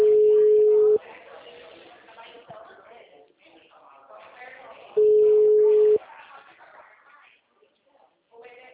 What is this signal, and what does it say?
Ringback tone (Europe, 425 Hz), with speech.